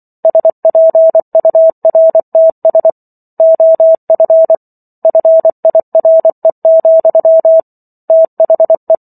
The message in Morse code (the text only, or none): SPURTS OF FIRE, THE